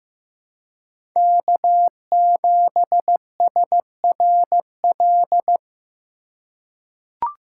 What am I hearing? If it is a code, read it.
Morse code, heard as a keyed tone, about 15 wpm, 700 Hz. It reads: K7SRL
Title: USA Call Signs - 5 Characters - 4x Character Spacing - 15wpm